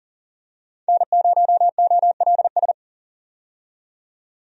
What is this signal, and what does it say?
Morse code, heard as a keyed tone, about 40 wpm, 700 Hz. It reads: N0OLS
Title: USA Call Signs - 5 Characters - 40wpm